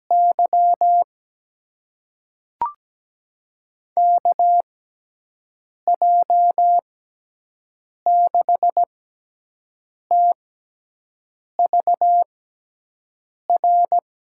Morse code, heard as a keyed tone, about 17 wpm, spaced as for 6 wpm, 700 Hz. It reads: Y KJ6TVR
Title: USA Call Signs - 6 Characters - 6x Character Spacing - 17wpm